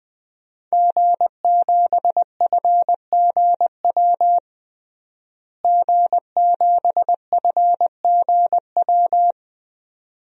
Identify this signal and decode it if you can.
Morse code, heard as a keyed tone, about 20 wpm, 700 Hz. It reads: G7FGW G7FGW
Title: DXCC-100 Call Signs - 5 Characters - Repeated 2x - 20wpm